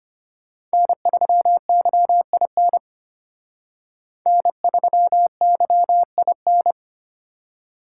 Morse code, heard as a keyed tone, about 29 wpm, 700 Hz. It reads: N3YIN N3YIN
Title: Speed Racing - USA 1x3 Call Signs 25wpm